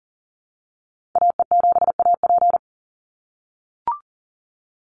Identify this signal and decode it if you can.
Morse code, heard as a keyed tone, about 38 wpm, 700 Hz. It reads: AE7AP
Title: NA SOTA Activator Callsigns 40wpm